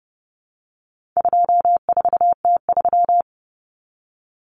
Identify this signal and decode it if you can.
Morse code, heard as a keyed tone, about 30 wpm, 700 Hz. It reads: 24T3
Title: Common Cut Numbers - 4-Digit 30wpm